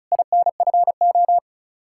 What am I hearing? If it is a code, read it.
Morse code, heard as a keyed tone, about 34 wpm, 700 Hz. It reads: INFO
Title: Practice QSO - Exchange 3 - More Info - 35wpm